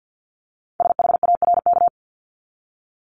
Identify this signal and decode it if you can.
Morse code, heard as a keyed tone, about 50 wpm, 690 Hz. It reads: SHARK
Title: Multi-letter Phonemes 50wpm